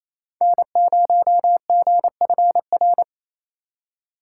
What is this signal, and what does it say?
Morse code, heard as a keyed tone, about 28 wpm, 700 Hz. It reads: N0GFR
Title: USA Call Signs - 5 Characters - 2x Character Spacing - 28wpm